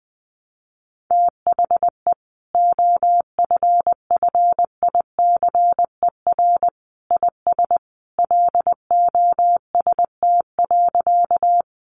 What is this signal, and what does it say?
Morse code, heard as a keyed tone, about 19 wpm, 700 Hz. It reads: THE OFFICER IS LOST.
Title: Sentences from Top 1400 Words 20wpm